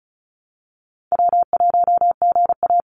Morse code, heard as a keyed tone, about 34 wpm, 700 Hz. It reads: W1GA
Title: USA Call Signs 35wpm